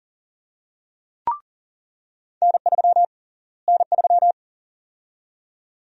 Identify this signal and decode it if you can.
Morse code, heard as a keyed tone, about 40 wpm, 700 Hz. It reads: N3 N3
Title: USA Call Sign Prefixes - Repeated 2x - 40wpm